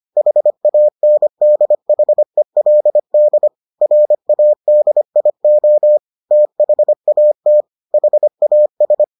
Morse code, heard as a keyed tone, about 25 wpm, 600 Hz. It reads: HANDHELD RADIO THAT HAS